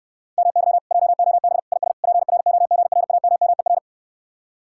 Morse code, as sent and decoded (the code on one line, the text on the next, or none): -- -.-- / .--- --- -... / .. ... / .--. .-. --- --. .-. .- -- -- . .-.
MY JOB IS PROGRAMMER